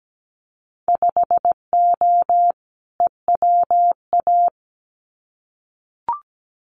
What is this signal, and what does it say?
Morse code, heard as a keyed tone, about 17 wpm, 700 Hz. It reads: 5O EWA
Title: Winter Field Day Exchanges Encore 17wpm